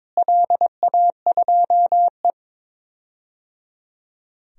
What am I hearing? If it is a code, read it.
Morse code, heard as a keyed tone, about 22 wpm, 700 Hz. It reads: LA2E
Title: DXCC-100 Call Signs - 4 Characters - Repeated 3x - 22wpm